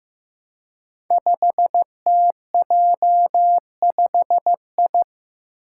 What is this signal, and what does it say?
Morse code, heard as a keyed tone, about 15 wpm, 700 Hz. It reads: 5TJ5I
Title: Five Challenging Letter-Number Combinations 15wpm